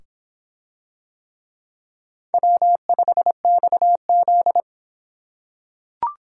Morse code, heard as a keyed tone, about 24 wpm, 700 Hz. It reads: W5XZ